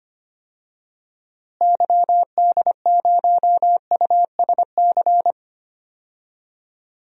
Morse code, heard as a keyed tone, about 25 wpm, 700 Hz. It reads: YD0USC
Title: DXCC-100 Call Signs - 6 Characters - Repeated 2x - 25wpm